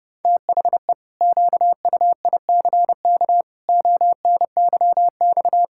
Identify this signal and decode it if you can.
Morse code, heard as a keyed tone, about 30 wpm, 700 Hz. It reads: THE QUICK ONYX
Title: Pangram Challenge v2 - 30wpm